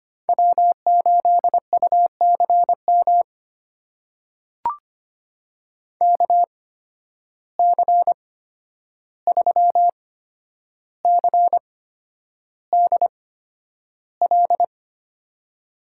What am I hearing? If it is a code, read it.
Morse code, heard as a keyed tone, about 25 wpm, spaced as for 10 wpm, 700 Hz. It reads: W8UCM KC3CDL
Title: USA Call Signs - 8x Character Spacing - 25wpm